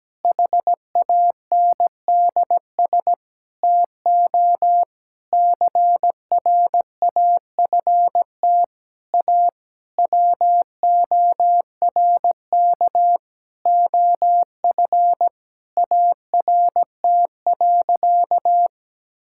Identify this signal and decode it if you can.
Morse code, heard as a keyed tone, about 17 wpm, 700 Hz. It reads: HANDS TO CRAFT A WORK OF ART.
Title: Sentences from Top 1000 Words - Review Part 6 - 17wpm